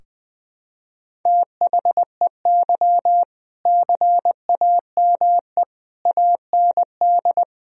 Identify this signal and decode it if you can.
Morse code, heard as a keyed tone, about 20 wpm, 700 Hz. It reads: THEY CAME AND
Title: Sentences from Top 1000 Words - Review Part 3 - 20wpm